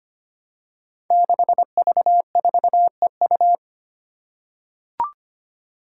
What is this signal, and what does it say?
Morse code, heard as a keyed tone, about 25 wpm, 700 Hz. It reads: 6V4EU
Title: Five Challenging Letter-Number Combinations 25wpm